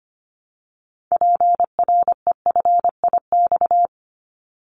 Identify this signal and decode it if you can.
Morse code, heard as a keyed tone, about 25 wpm, 700 Hz. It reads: PREFIX
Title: Callsign Prefixes for Top 100 DXCC - 25wpm